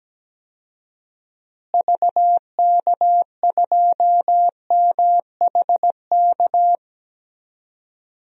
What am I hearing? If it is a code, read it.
Morse code, heard as a keyed tone, about 17 wpm, 700 Hz. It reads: VK2MHK
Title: DXCC-100 Call Signs - 6 Characters - Repeated 2x - 17wpm